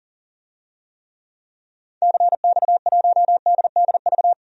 Morse code, heard as a keyed tone, about 40 wpm, 700 Hz. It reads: CX1DDV
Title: DXCC-100 Call Signs - 6 Characters - 40wpm